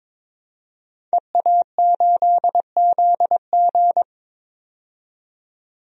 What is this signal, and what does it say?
Morse code, heard as a keyed tone, about 22 wpm, 700 Hz. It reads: EA8ZG
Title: DXCC-100 Call Signs - 5 Characters - 22wpm